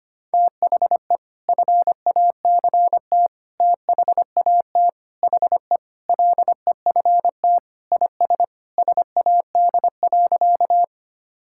Morse code, heard as a keyed tone, about 25 wpm, 700 Hz. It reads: THE FACT THAT HE LEFT IS SAD.
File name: practice / 25wpm / Sentences with 2-Word Phrases - 25wpm